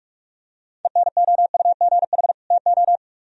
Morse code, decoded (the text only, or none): ENOUGH TO